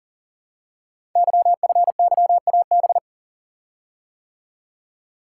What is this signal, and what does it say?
Morse code, heard as a keyed tone, about 39 wpm, 700 Hz. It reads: YFYAB
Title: Code Groups - 5 Letter:Numbers - 40wpm